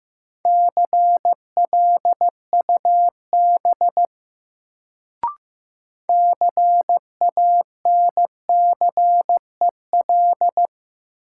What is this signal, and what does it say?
Morse code, heard as a keyed tone, about 15 wpm, 700 Hz. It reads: CLUB CANCEL
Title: Sentences from Top 1200 Words - New Word Drill - 15wpm